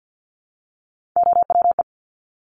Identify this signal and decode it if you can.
Morse code, heard as a keyed tone, about 50 wpm, 700 Hz. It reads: OWE
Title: Top Three Letter Combinations in Words 50wpm